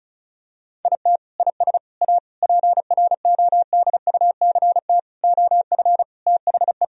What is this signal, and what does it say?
Morse code, heard as a keyed tone, about 34 wpm, 700 Hz. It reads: IT IS A PRODUCT OF THE
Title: Sentences from Top 1000 Words - Review Part 2 - 35wpm